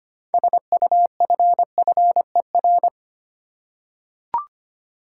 Morse code, heard as a keyed tone, about 25 wpm, 700 Hz. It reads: SUFFER
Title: Six Letter Words - 25wpm